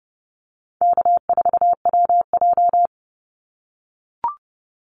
Morse code, heard as a keyed tone, about 30 wpm, 700 Hz. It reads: K4WJ
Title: Easy Call Signs - 30wpm